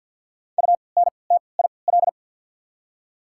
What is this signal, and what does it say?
Morse code, heard as a keyed tone, about 53 wpm, spaced as for 29 wpm, 700 Hz. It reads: UNTIL
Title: 3x Character Spacing - Top 500 Words - 50wpm